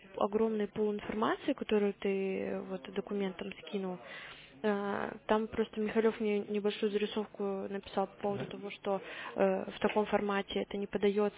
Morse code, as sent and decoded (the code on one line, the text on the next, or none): none